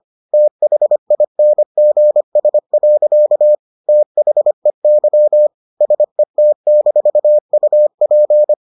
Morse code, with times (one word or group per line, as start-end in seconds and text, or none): 0.3-3.5 THINGS.
3.9-5.5 THEY
5.8-8.5 SET-UP